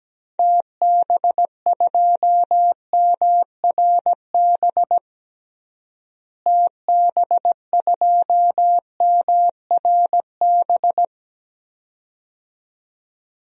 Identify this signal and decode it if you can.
Morse code, heard as a keyed tone, about 17 wpm, 700 Hz. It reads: TB2MRB TB2MRB
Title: DXCC-100 Call Signs - 6 Characters - Repeated 2x - 17wpm